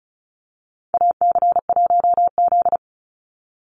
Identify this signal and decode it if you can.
Morse code, heard as a keyed tone, about 36 wpm, 700 Hz. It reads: AC1Z
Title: NA SOTA Activator Callsigns 35wpm